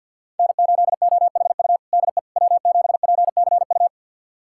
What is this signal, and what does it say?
Morse code, heard as a keyed tone, about 46 wpm, 700 Hz. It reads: N8OHU DE W7PCU
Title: USA Callsign DE Callsign - 5 Characters - 50wpm